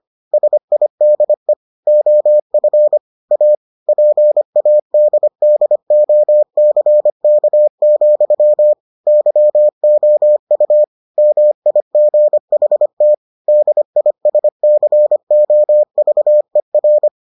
Morse code, heard as a keyed tone, about 25 wpm, 600 Hz. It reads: SIDE OF A PADDOCK, YOU MIGHT DISCOVER